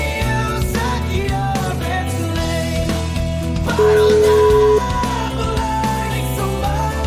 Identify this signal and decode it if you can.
Ringback tone (Europe, 425 Hz), with noise.